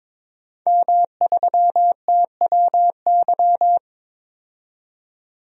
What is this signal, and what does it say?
Morse code, heard as a keyed tone, about 22 wpm, 700 Hz. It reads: M3TWY
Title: DXCC-100 Call Signs - 5 Characters - 22wpm